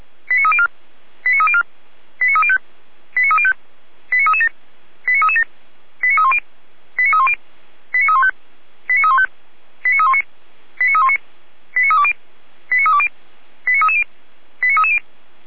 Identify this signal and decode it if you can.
Five-tone selective call (ZVEI), heard as quick runs of five tones, tones of about 70 ms. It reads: 89304 89304 89305 89305 89337 89337 89310 89310 89315 89315 89319 89319 89320 89320 89330 89330
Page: /455/SDR/ZVEI/